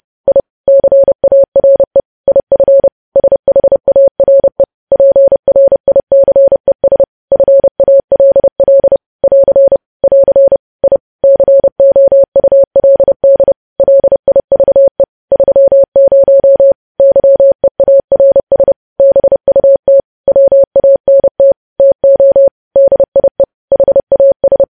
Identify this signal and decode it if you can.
Morse code, heard as a keyed tone, about 30 wpm, 550 Hz. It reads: I CARE IF SHARE PRICES FALL <AR> <AR> I COULD LIVE 30 YEARS BUT WANT TO DIE HAS
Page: News Headlines in Morse Code at 30 WPM